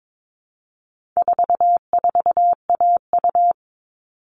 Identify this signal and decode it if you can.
Morse code, heard as a keyed tone, about 22 wpm, 700 Hz. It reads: 44AU
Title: All Cut Numbers - 4-Digit 22wpm